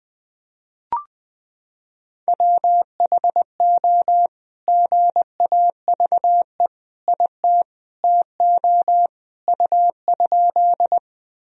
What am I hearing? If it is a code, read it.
Morse code, heard as a keyed tone, about 20 wpm, 700 Hz. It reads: WHO GAVE IT TO U?
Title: Sentences from Top 500 Words and Abbreviations 20wpm